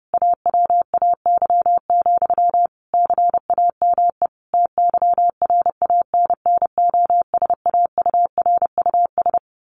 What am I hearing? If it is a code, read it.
Morse code, heard as a keyed tone, about 30 wpm, 700 Hz. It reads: AWAY, CAME TYRANNOSAURUS
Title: A Sound Of Thunder 30wpm